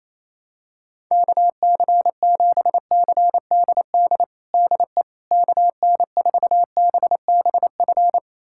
Morse code, heard as a keyed tone, about 28 wpm, 700 Hz. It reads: KC7CDD DE KN4BBF
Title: USA Callsign DE Callsign - 6 Characters - 28wpm